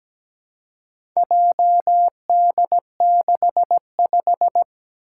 Morse code, heard as a keyed tone, about 17 wpm, 700 Hz. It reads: JD65
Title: Four Challenging Letter-Number Combinations 17wpm